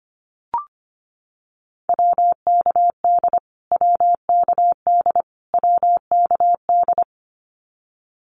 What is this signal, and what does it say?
Morse code, heard as a keyed tone, about 25 wpm, 700 Hz. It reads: WKD WKD WKD
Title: Abbreviations Repeated 3x - 25wpm